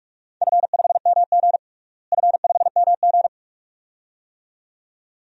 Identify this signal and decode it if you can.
Morse code, heard as a keyed tone, about 45 wpm, 700 Hz. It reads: F5MG F5MG
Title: DXCC-100 Call Signs - 4 Characters - Repeated 2x - 45wpm